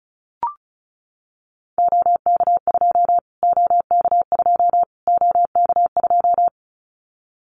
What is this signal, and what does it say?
Morse code, heard as a keyed tone, about 34 wpm, 700 Hz. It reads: OK2 OK2 OK2
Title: Callsign Prefixes for Top 25 DXCC - Repeated 3x - 35wpm